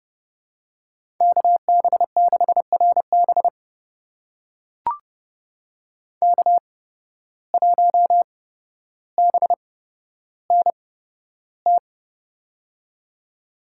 Morse code, heard as a keyed tone, about 29 wpm, spaced as for 13 wpm, 700 Hz. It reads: KB6RB K1BNT
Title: USA Call Signs - 5 Characters - 8x Character Spacing - 30wpm